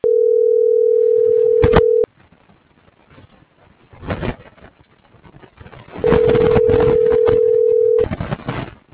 Ringback tone (North America), 440 + 480 Hz.